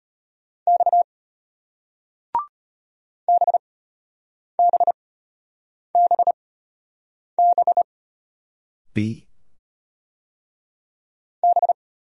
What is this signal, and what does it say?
Morse code, heard as a keyed tone, about 33 wpm, spaced as for 8 wpm, 700 Hz. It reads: X BBBB B